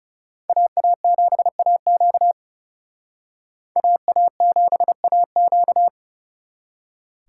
Morse code, heard as a keyed tone, about 38 wpm, 700 Hz. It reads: AA7AQ AA7AQ